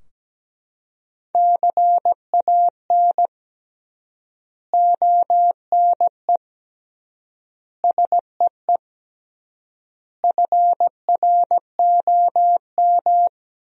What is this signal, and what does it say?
Morse code, heard as a keyed tone, about 17 wpm, 700 Hz. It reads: CAN ONE SEE FROM
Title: 3x Word-Spacing - Sentences from Top 100 Words - 17wpm